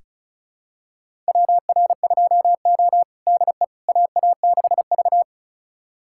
Morse code, heard as a keyed tone, about 36 wpm, 700 Hz. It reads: WR2O DE AA6V
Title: USA Callsign DE Callsign - 4 Characters - 35wpm